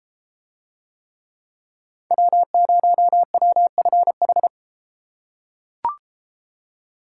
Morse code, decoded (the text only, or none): W0WFH